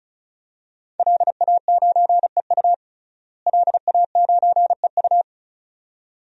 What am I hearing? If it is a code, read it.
Morse code, heard as a keyed tone, about 34 wpm, 700 Hz. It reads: LA9EU LA9EU
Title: DXCC-100 Call Signs - 5 Characters - Repeated 2x - 35wpm